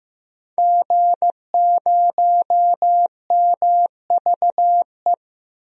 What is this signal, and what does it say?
Morse code, heard as a keyed tone, about 15 wpm, 700 Hz. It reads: G0MVE
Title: DXCC-100 Call Signs - 5 Characters - Repeated 3x - 15wpm